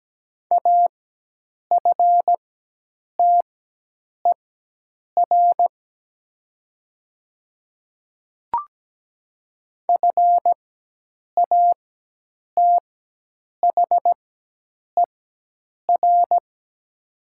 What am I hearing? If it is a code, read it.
Morse code, heard as a keyed tone, about 17 wpm, spaced as for 8 wpm, 700 Hz. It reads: AFTER FATHER
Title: 4x Character Spacing - Top 200 Words - 17wpm